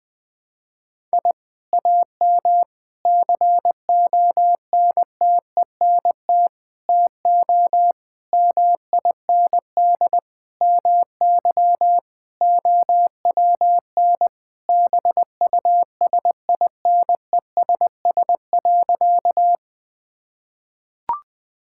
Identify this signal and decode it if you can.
Morse code, heard as a keyed tone, about 21 wpm, 700 Hz. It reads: I AM CONTENT TO MIND MY OWN BUSINESS.
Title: Sentences from Top 1000 Words - Review Part 6 - 20wpm